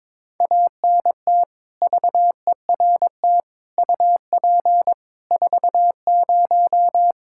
Morse code, heard as a keyed tone, about 22 wpm, 700 Hz. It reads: ANT VERT UP 40
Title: Practice QSO - Exchange 2 - Info - 22wpm